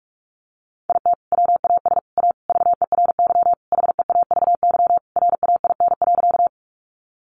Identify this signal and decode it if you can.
Morse code, heard as a keyed tone, about 44 wpm, 700 Hz. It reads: IT WAS A VERY HEAVY RAIN.
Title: Sentences from Top 1100 Words 45wpm